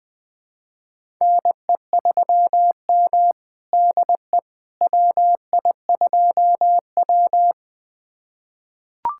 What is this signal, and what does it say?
Morse code, heard as a keyed tone, about 20 wpm, 700 Hz. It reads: NE3M DE WI2W
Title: USA Callsign DE Callsign - 4 Characters - 20wpm